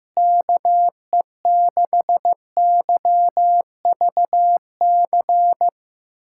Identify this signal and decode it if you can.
Morse code, heard as a keyed tone, about 15 wpm, 700 Hz. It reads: KE6YVC
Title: USA Callsign DE Callsign - 6 Characters - 15wpm